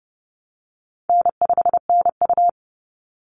Morse code, heard as a keyed tone, about 30 wpm, 700 Hz. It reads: N5NU
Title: Easy Call Signs Repeated 2x - 30wpm